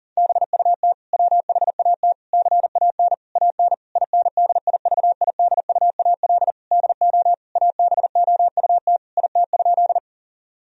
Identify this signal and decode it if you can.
Morse code, heard as a keyed tone, about 37 wpm, 700 Hz. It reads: BUT WHAT CAN AN INDIVIDUAL DO ABOUT IT?
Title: Sentences from Top 1000 Words - Review Part 4 - 40wpm